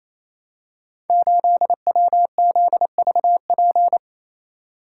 Morse code, heard as a keyed tone, about 27 wpm, 700 Hz. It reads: 8WZVP
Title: Code Groups - 5 Letter:Numbers - 28wpm